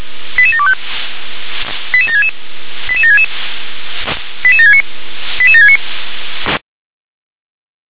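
Five-tone selective call (ZVEI), heard as quick runs of five tones, tones of about 70 ms. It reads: 88725 88766 88766 88769 88769
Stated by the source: ZVEI five-tone call 88756 received on SDR.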